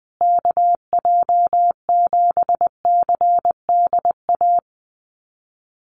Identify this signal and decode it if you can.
Morse code, heard as a keyed tone, about 20 wpm, 700 Hz. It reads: KJ7CDA
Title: USA Call Signs Repeated 2x 20wpm